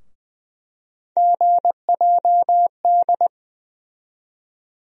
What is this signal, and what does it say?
Morse code, heard as a keyed tone, about 20 wpm, 700 Hz. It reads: GJD
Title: USA Call Sign Postfixes - 3 Letters 20wpm